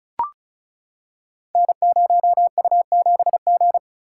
Morse code, heard as a keyed tone, about 36 wpm, 700 Hz. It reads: N0U7G
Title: Code Groups - 5 Letter:Numbers - 35wpm